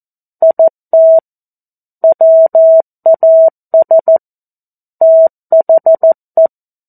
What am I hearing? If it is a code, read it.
Morse code, heard as a keyed tone, about 14 wpm, 650 Hz. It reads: IT WAS THE
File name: 02-Chapter-02 14 WPM